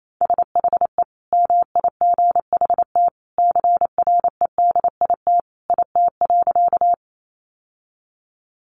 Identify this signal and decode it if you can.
Morse code, heard as a keyed tone, about 28 wpm, 700 Hz. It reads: SHE MIGHT CREDIT IT.
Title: Sentences from Top 1000 Words 28wpm